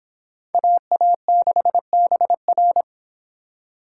Morse code, heard as a keyed tone, about 25 wpm, 700 Hz. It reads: AA6BR